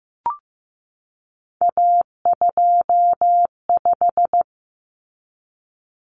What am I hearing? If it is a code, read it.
Morse code, heard as a keyed tone, about 15 wpm, 700 Hz. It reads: A25
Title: Common Cut Numbers - 3-Digit 15wpm